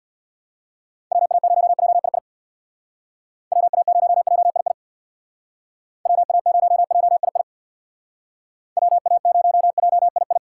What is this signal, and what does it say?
Morse code, heard as a keyed tone, about 62 wpm, 700 Hz. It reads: WA0JII WA0JII WA0JII WA0JII
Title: Speed Racing - USA Call Signs 50wpm